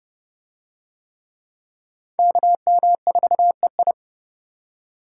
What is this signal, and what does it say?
Morse code, heard as a keyed tone, about 29 wpm, 700 Hz. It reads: KM4EI